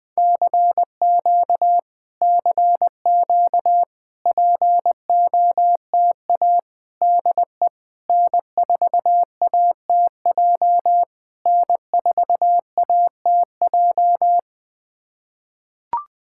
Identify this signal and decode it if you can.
Morse code, heard as a keyed tone, about 21 wpm, 700 Hz. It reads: CQ CQ POTA DE N4ATJ N4ATJ
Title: POTA QSO Practice - 20wpm